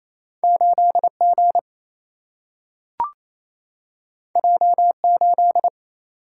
Morse code, heard as a keyed tone, about 28 wpm, 700 Hz. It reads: 8 G J 8
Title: Single-Letter Single-Number 28wpm